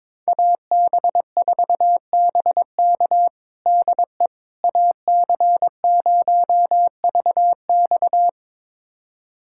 Morse code, heard as a keyed tone, about 21 wpm, 700 Hz. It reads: AB4BK DE AC0VX
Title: USA Callsign DE Callsign - 5 Characters - 22wpm